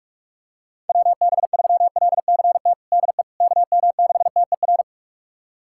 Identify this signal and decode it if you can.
Morse code, heard as a keyed tone, about 46 wpm, 700 Hz. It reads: WB3LCT DE KM6TER
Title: USA Callsign DE Callsign 45wpm